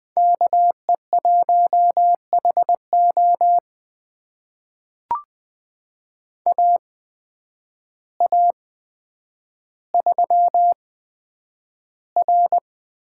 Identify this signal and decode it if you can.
Morse code, heard as a keyed tone, about 21 wpm, spaced as for 9 wpm, 700 Hz. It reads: KE1HO AA3R
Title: USA Call Signs - 5 Characters - 8x Character Spacing - 20wpm